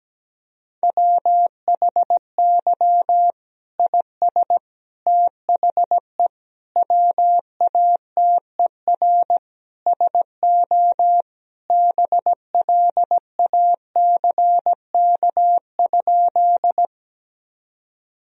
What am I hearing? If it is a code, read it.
Morse code, heard as a keyed tone, about 17 wpm, 700 Hz. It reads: WHY IS THE WATER SO BLACK?
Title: Sentences from Top 1000 Words - Review Part 2 - 17wpm